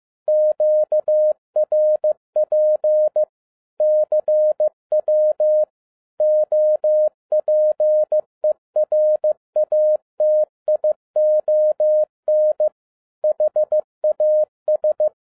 Morse code, heard as a keyed tone, about 15 wpm, 610 Hz. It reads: QRP CW OPERATION HAS
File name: 2011-07-mp3 / n42 EXCITING QRP CW OPERATION 15wpm